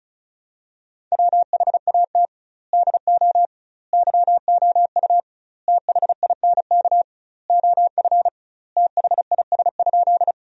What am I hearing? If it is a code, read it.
Morse code, heard as a keyed tone, about 36 wpm, 700 Hz. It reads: WHAT DO YOU THINK OF THIS?
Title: 2x Word-Spacing - Sentences from Top 200 Words - 35wpm